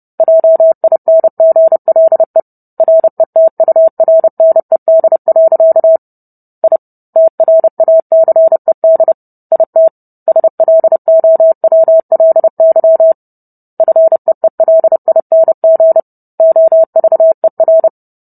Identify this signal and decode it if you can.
Morse code, heard as a keyed tone, about 30 wpm, 650 Hz. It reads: JINGLE RETURNED. I TRACED IT SLOWLY FEELING OVER